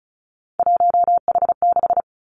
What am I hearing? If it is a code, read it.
Morse code, heard as a keyed tone, about 36 wpm, 700 Hz. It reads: 1H6
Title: Three Challenging Letter-Number Combinations 35wpm